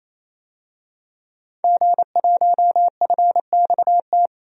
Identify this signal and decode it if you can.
Morse code, heard as a keyed tone, about 28 wpm, 700 Hz. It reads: G1FXT
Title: DXCC-100 Call Signs - 5 Characters - 28wpm